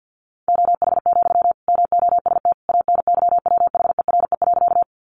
Morse code, heard as a keyed tone, about 50 wpm, 700 Hz. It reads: OH, MOST ANYWHERE.
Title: Sentences from Top 1400 Words 50wpm